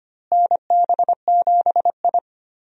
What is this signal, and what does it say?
Morse code, heard as a keyed tone, about 25 wpm, 700 Hz. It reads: NB7I
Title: USA Call Signs - 4 Characters - Repeated 3x - 25wpm